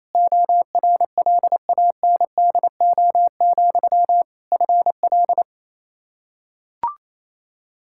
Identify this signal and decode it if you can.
Morse code, heard as a keyed tone, about 28 wpm, 700 Hz. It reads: ORLANDO, FL
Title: Practice QSO - Exchange 1 - Beginning - 28wpm